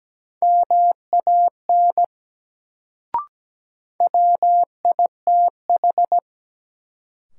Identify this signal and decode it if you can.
Morse code, heard as a keyed tone, about 17 wpm, 700 Hz. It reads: MAN WITH
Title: Sets of 2 Words - New Word Drill - 17wpm